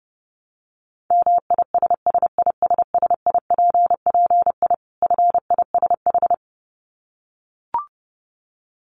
Morse code, heard as a keyed tone, about 30 wpm, 700 Hz. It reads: MISSISSIPPI FISH
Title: Dit Challenge 30wpm